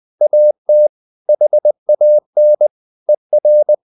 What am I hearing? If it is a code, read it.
Morse code, heard as a keyed tone, about 20 wpm, 600 Hz. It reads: AT HAN ER
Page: Ferie 20 wpm